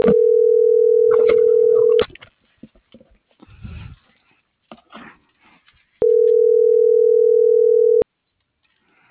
Ringback tone (North America), 440 + 480 Hz.